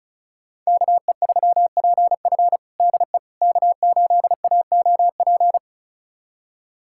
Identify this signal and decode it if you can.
Morse code, heard as a keyed tone, about 34 wpm, 700 Hz. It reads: KE3PF DE K8AOP
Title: USA Callsign DE Callsign - 5 Characters - 35wpm